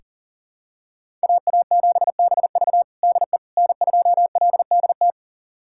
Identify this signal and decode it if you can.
Morse code, heard as a keyed tone, about 37 wpm, 700 Hz. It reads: AA7BV DE N2LDT
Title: USA Callsign DE Callsign - 5 Characters - 40wpm